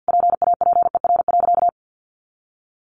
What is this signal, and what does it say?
Morse code, heard as a keyed tone, about 48 wpm, 700 Hz. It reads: PAPER.
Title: Sentences from Top 600 Words 50wpm